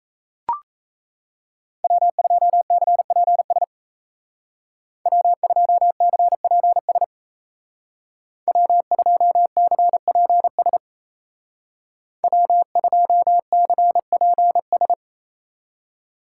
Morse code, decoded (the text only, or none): W2CPS W2CPS W2CPS W2CPS